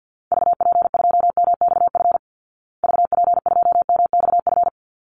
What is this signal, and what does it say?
Morse code, heard as a keyed tone, about 50 wpm, 700 Hz. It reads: VP2MXF VP2MXF
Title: Call Signs Repeated 3x - 50wpm